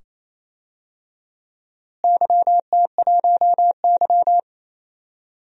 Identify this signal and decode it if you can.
Morse code, heard as a keyed tone, about 28 wpm, 700 Hz. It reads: YT1Y
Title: DXCC-100 Call Signs - 4 Characters - Repeated 2x - 28wpm